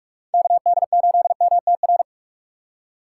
Morse code, heard as a keyed tone, about 47 wpm, 700 Hz. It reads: KD8MTR
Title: USA Call Signs - 6 Characters - 4x Character Spacing - 45wpm